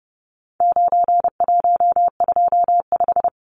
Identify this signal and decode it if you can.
Morse code, heard as a keyed tone, about 30 wpm, 700 Hz. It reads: 9125
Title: Numbers - 4-Digit 30wpm